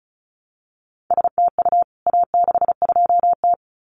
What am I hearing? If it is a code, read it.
Morse code, heard as a keyed tone, about 34 wpm, 700 Hz. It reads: STU A62T
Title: CWT Exchange 35wpm